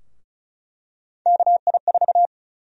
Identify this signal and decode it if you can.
Morse code, heard as a keyed tone, about 36 wpm, 700 Hz. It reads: KI4
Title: USA Call Sign Prefixes - 35wpm